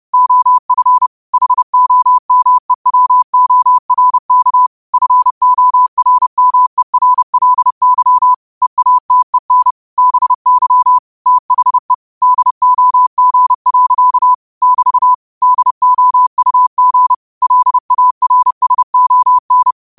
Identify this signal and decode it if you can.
Morse code, heard as a keyed tone, about 30 wpm, 1000 Hz. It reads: OF HOMEWORK FORMERLY EATEN BY THE DOG. <BT> DOUG LARSON
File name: qotdfiles / 30wpm